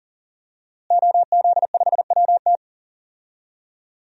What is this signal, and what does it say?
Morse code, heard as a keyed tone, about 40 wpm, 700 Hz. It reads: OZ5WT
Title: DXCC-100 Call Signs - 5 Characters - Repeated 3x - 40wpm